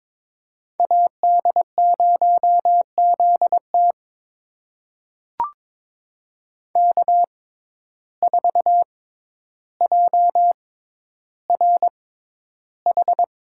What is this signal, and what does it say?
Morse code, heard as a keyed tone, about 22 wpm, spaced as for 11 wpm, 700 Hz. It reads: AD0ZT K4JRH
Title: USA Call Signs - 5 Characters - 6x Character Spacing - 22wpm